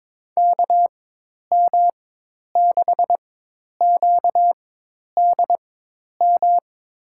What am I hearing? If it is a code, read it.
Morse code, heard as a keyed tone, about 23 wpm, spaced as for 10 wpm, 700 Hz. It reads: KM6QDM
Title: USA Call Signs - 6 Characters - 4x Character Spacing - 22wpm